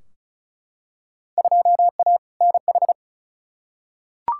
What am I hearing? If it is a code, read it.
Morse code, heard as a keyed tone, about 36 wpm, 700 Hz. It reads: 2A NH
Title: Field Day Exchanges Encore - 35wpm